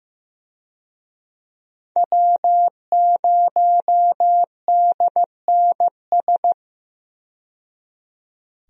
Morse code, heard as a keyed tone, about 15 wpm, 700 Hz. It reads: W0DNS
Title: USA Call Signs - 5 Characters - 15wpm